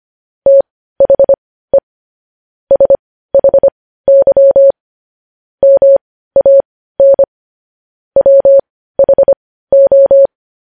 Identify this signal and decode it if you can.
Morse code, heard as a keyed tone, about 25 wpm, spaced as for 15 wpm, 550 Hz. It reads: THE SHY MAN WHO
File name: podcast / News headlines-1x-25x15